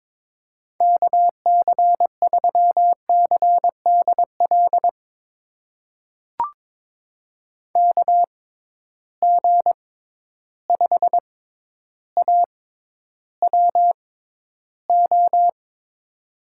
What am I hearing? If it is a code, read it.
Morse code, heard as a keyed tone, about 22 wpm, spaced as for 11 wpm, 700 Hz. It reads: KC3CDL KG5AWO
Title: USA Call Signs - 6 Characters - 6x Character Spacing - 22wpm